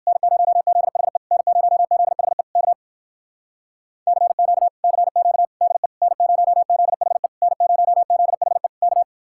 Morse code, heard as a keyed tone, about 59 wpm, 700 Hz. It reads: N0ZHE N0ZHE K CQ CQ DE N0ZHE N0ZHE K